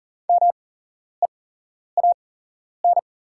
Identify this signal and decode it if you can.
Morse code, heard as a keyed tone, about 40 wpm, spaced as for 11 wpm, 700 Hz. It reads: MEAN
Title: 8x Character Spacing - Top 200 Words - 40wpm